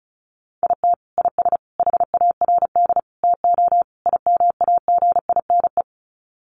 Morse code, heard as a keyed tone, about 34 wpm, 700 Hz. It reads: IT IS HARD TO IMAGINE
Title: Sets of 5 Words Encore2 35wpm